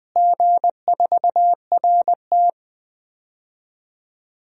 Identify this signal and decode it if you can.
Morse code, heard as a keyed tone, about 20 wpm, 700 Hz. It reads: G4RT
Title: DXCC-100 Call Signs - 4 Characters - Repeated 2x - 20wpm